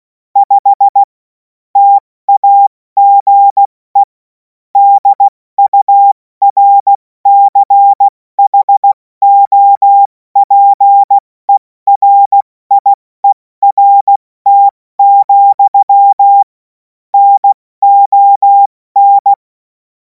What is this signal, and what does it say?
Morse code, heard as a keyed tone, about 16 wpm, spaced as for 14 wpm, 800 Hz. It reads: 5 TAGE DURCHOPERIERT, NON
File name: Morsekurs / Witze / Witz 11. August 2026 70 F80